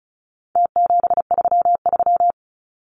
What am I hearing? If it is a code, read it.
Morse code, heard as a keyed tone, about 34 wpm, 700 Hz. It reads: T733
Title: Numbers - Contest Style - 4-Digit 35wpm